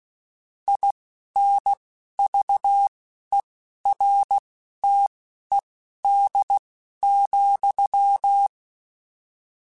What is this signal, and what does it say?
Morse code, heard as a keyed tone, about 16 wpm, spaced as for 11 wpm, 800 Hz. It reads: INVERTED,
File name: Technical Subjects / Technical Subjects-mp3s / XC @10